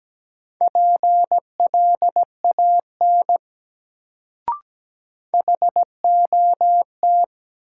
Morse code, heard as a keyed tone, about 17 wpm, 700 Hz. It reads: PLAN HOT
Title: Sentences from Top 500 Words - New Word Drill - 17wpm